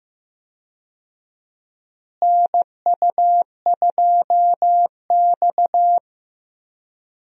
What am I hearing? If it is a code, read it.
Morse code, heard as a keyed tone, about 16 wpm, 700 Hz. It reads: NU2X
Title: USA Call Signs - 4 Characters - Repeated 3x - 15wpm